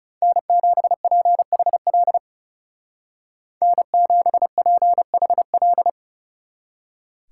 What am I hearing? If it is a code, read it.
Morse code, heard as a keyed tone, about 35 wpm, 700 Hz. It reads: N7PHL N7PHL
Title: Speed Racing - USA 1x3 Call Signs 30wpm